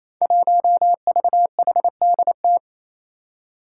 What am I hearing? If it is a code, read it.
Morse code, heard as a keyed tone, about 28 wpm, 700 Hz. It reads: 1VHDT
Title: Five Challenging Letter-Number Combinations 28wpm